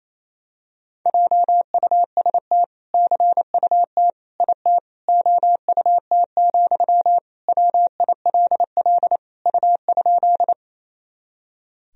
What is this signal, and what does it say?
Morse code, heard as a keyed tone, about 28 wpm, 700 Hz. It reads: JUST CUT IT OUT, WILL U?
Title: Sentences from Top 300 Words and Abbreviations 28wpm